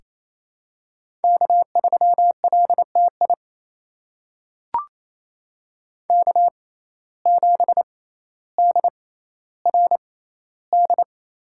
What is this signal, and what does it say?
Morse code, heard as a keyed tone, about 28 wpm, spaced as for 14 wpm, 700 Hz. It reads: K3LTI K7DRD
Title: USA Call Signs - 5 Characters - 6x Character Spacing - 28wpm